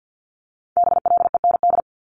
Morse code, heard as a keyed tone, about 50 wpm, 690 Hz. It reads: BLEND
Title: Multi-letter Phonemes 50wpm